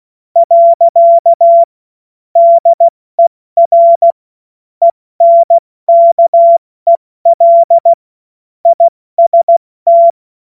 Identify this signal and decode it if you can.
Morse code, heard as a keyed tone, about 16 wpm, 680 Hz. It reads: . DER ENKEL IST
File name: Morsekurs / Witze / Witz 19. Oktober 2021, 70 F80